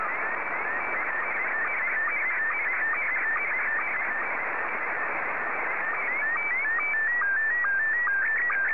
SSTV (slow-scan television), Scottie 1.